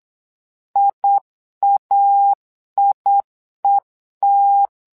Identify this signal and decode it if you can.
Morse code, heard as a keyed tone, about 8 wpm, 800 Hz. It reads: IAIET
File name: LEZIONI / 2 UHDN / 2 UH 8wpm 2129